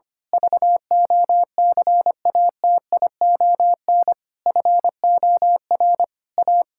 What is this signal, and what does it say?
Morse code, heard as a keyed tone, about 25 wpm, 700 Hz. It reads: VOCATION FOR A
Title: Pangram Challenge v2 - 25wpm